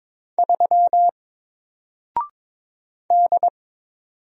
Morse code, heard as a keyed tone, about 22 wpm, 700 Hz. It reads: 3 D
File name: Single-Letter-Number-22wpm